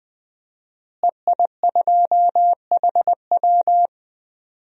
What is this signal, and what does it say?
Morse code, heard as a keyed tone, about 20 wpm, 700 Hz. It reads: EI2HW
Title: DXCC-100 Call Signs - 5 Characters - Repeated 3x - 20wpm